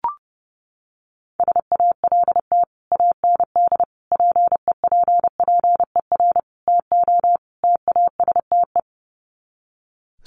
Morse code, heard as a keyed tone, about 30 wpm, 700 Hz. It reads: SALT AND PEPPER TO TASTE
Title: Sets of 5 Words 30wpm